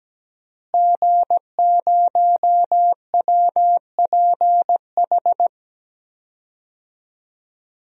Morse code, heard as a keyed tone, about 17 wpm, 700 Hz. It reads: G0WPH
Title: DXCC-100 Call Signs - 5 Characters - Repeated 3x - 17wpm